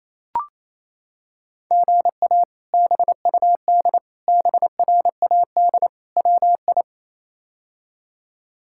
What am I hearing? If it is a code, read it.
Morse code, heard as a keyed tone, about 27 wpm, 700 Hz. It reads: GA BUD BRAD WI
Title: K1USN SST QSO Practice - 28wpm